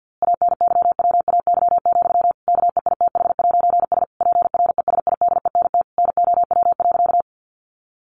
Morse code, heard as a keyed tone, about 50 wpm, 700 Hz. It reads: ANYWAY, KEITH'S PRESIDENT NOW.
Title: A Sound Of Thunder 50wpm